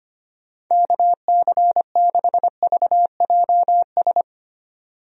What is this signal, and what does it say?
Morse code, heard as a keyed tone, about 25 wpm, 700 Hz. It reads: KC6VJS
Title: USA Call Signs - 6 Characters - Repeated 2x - 25wpm